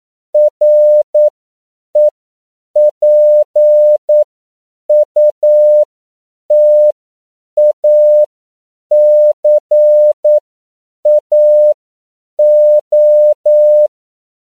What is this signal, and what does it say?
Morse code, heard as a keyed tone, about 9 wpm, spaced as for 7 wpm, 600 Hz. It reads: REPUTACAO
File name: CW / 10PPM